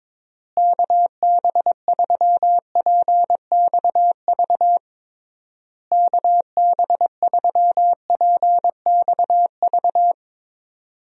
Morse code, heard as a keyed tone, about 22 wpm, 700 Hz. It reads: KB3PXV KB3PXV
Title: USA Call Signs - 6 Characters - Repeated 3x - 22wpm